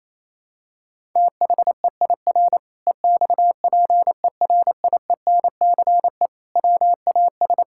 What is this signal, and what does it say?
Morse code, heard as a keyed tone, about 28 wpm, 700 Hz. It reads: THEIR EXPERIENCE WAS
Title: Sentences from Top 1000 Words - Review Part 5 - 28wpm